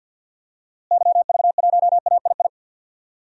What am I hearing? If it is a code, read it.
Morse code, heard as a keyed tone, about 51 wpm, 700 Hz. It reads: YV1AII
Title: DXCC-100 Call Signs - 6 Characters - Repeated 2x - 50wpm